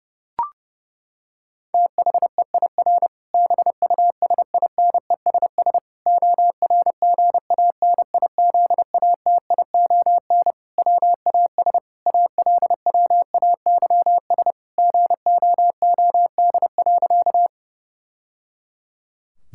Morse code, heard as a keyed tone, about 30 wpm, 700 Hz. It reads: THEIR BUSINESS ORGANIZATION WAS ALWAYS GOOD.
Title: Sentences from Top 1000 Words - Review Part 5 - 30wpm